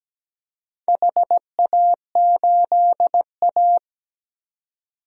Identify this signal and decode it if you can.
Morse code, heard as a keyed tone, about 17 wpm, 700 Hz. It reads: HA8A
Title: DXCC-100 Call Signs - 4 Characters - 17wpm